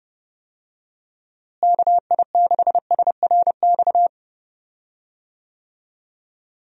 Morse code, heard as a keyed tone, about 30 wpm, 700 Hz. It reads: KI6SRX
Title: USA Call Signs - 6 Characters - Repeated 2x - 30wpm